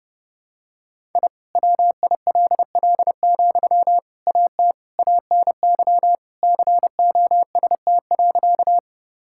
Morse code, heard as a keyed tone, about 32 wpm, 700 Hz. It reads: I WILL, AT ANY COST.
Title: Sentences from Top 1000 Words - Review Part 5 - 30wpm